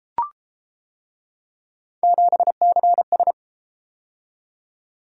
Morse code, heard as a keyed tone, about 31 wpm, 700 Hz. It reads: 7CS